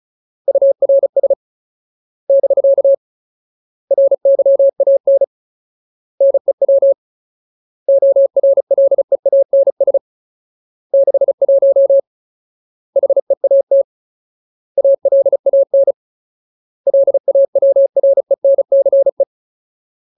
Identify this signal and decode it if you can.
Morse code, heard as a keyed tone, about 34 wpm, 550 Hz. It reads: URS <BK> RYAN NEW ORLEANS 61 HEAT ALAN LAWRENCE
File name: Ultimate Word List / 35WPM Ultimate